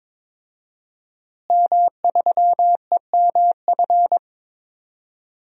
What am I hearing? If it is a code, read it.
Morse code, heard as a keyed tone, about 22 wpm, 700 Hz. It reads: M3EMF
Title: DXCC-100 Call Signs - 5 Characters - Repeated 3x - 22wpm